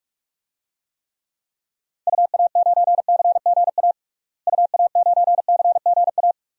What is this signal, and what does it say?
Morse code, heard as a keyed tone, about 42 wpm, 700 Hz. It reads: UA9CGA UA9CGA
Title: DXCC-100 Call Signs - 6 Characters - Repeated 3x - 45wpm